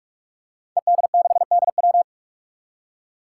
Morse code, heard as a keyed tone, about 47 wpm, 700 Hz. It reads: ED6DW
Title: Five Challenging Letter-Number Combinations 45wpm